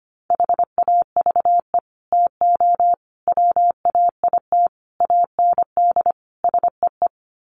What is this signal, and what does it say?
Morse code, heard as a keyed tone, about 25 wpm, 700 Hz. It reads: HAVE TO WAIT AND SEE
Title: Sets of 5 Words Encore2 25wpm